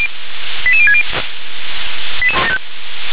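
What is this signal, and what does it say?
Five-tone selective call (ZVEI), heard as quick runs of five tones, tones of about 70 ms. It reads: 88060
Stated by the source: ZVEI five-tone selective call received on SDR.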